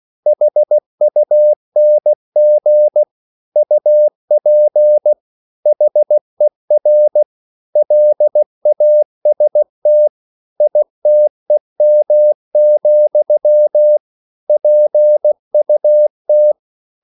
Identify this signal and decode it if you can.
Morse code, heard as a keyed tone, about 16 wpm, 600 Hz. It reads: HUNG UP HER LAST ITEM, PUT